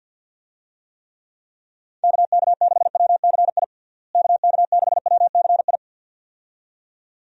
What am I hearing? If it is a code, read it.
Morse code, heard as a keyed tone, about 50 wpm, 700 Hz. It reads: KK6WCI KK6WCI
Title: USA Call Signs - 6 Characters - Repeated 2x - 50wpm